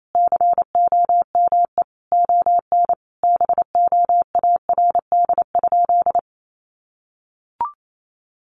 Morse code, heard as a keyed tone, about 28 wpm, 700 Hz. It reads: COME ON BOARD?
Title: Sentences from Top 700 Words 28wpm